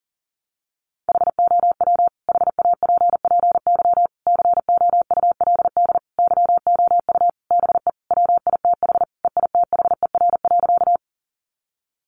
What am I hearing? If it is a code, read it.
Morse code, heard as a keyed tone, about 40 wpm, 700 Hz. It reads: HOW HAPPY COULD YOU BE WITH EITHER.
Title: Sentences from Top 1000 Words 40wpm